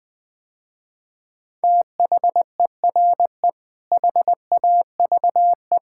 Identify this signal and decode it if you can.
Morse code, heard as a keyed tone, about 20 wpm, 700 Hz. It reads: THERE HAVE
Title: Sentences with 6-Word Phrases - Spoken Beginning - 20wpm